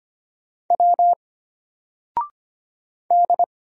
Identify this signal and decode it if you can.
Morse code, heard as a keyed tone, about 25 wpm, 700 Hz. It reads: W D
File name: Single-Letter-Number-25wpm